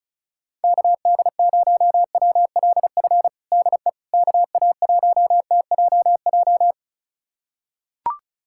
Morse code, heard as a keyed tone, about 34 wpm, 700 Hz. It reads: KD0WLF DE KA1TJJ
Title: USA Callsign DE Callsign - 6 Characters - 35wpm